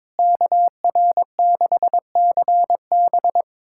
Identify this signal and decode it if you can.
Morse code, heard as a keyed tone, about 22 wpm, 700 Hz. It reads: KR6CB
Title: USA Callsign DE Callsign - 5 Characters - 22wpm